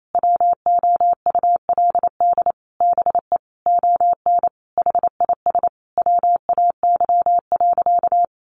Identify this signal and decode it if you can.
Morse code, heard as a keyed tone, about 28 wpm, 700 Hz. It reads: WOULD BE ON HIS WAY.
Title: Sentences from Top 300 Words 28wpm